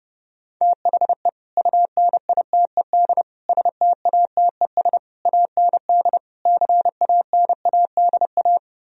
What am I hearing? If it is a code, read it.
Morse code, heard as a keyed tone, about 30 wpm, 700 Hz. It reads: THE UNITED STATES AND CANADA